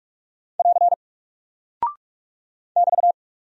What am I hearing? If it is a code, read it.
Morse code, heard as a keyed tone, about 45 wpm, 700 Hz. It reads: <AR> <BT>
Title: Single Character (Letter:Number:Punctuation:Prosign) Alt 45wpm